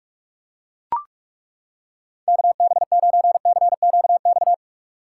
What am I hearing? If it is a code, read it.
Morse code, heard as a keyed tone, about 39 wpm, 700 Hz. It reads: KB9CQX